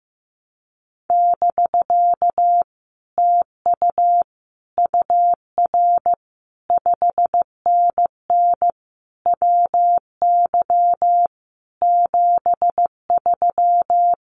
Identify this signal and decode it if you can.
Morse code, heard as a keyed tone, about 15 wpm, 700 Hz. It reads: <BK> TU UR 5NN WY 73
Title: Common QSO Phrases Encore 15wpm